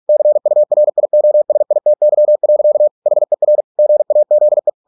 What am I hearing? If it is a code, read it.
Morse code, heard as a keyed tone, about 46 wpm, 600 Hz. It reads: CURIOSITY. HER GAZE